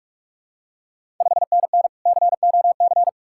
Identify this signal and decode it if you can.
Morse code, heard as a keyed tone, about 46 wpm, 690 Hz. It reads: 5NN COC
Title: Ontario QSO Party Exchange 45wpm